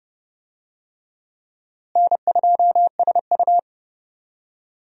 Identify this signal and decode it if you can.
Morse code, heard as a keyed tone, about 30 wpm, 700 Hz. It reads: N2SU
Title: USA Call Signs - 4 Characters - Repeated 3x - 30wpm